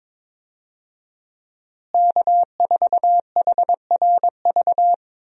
Morse code, heard as a keyed tone, about 22 wpm, 700 Hz. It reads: K4HRV
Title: USA Call Signs - 5 Characters - Repeated 2x - 22wpm